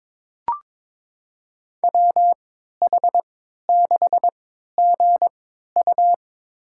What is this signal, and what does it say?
Morse code, heard as a keyed tone, about 22 wpm, spaced as for 13 wpm, 700 Hz. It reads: WH6GU
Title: USA Call Signs - 3x Character Spacing - 22wpm